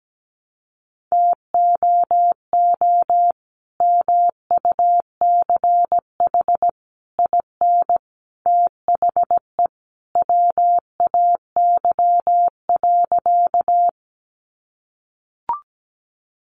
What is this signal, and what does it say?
Morse code, heard as a keyed tone, about 17 wpm, 700 Hz. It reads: TOO MUCH IN THE WAY.
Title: Sentences from Top 200 Words 17wpm